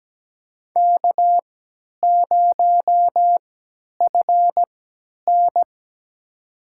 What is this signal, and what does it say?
Morse code, heard as a keyed tone, about 17 wpm, spaced as for 10 wpm, 700 Hz. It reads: K0FN
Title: USA Call Signs - 4 Characters - 3x Character Spacing - 17wpm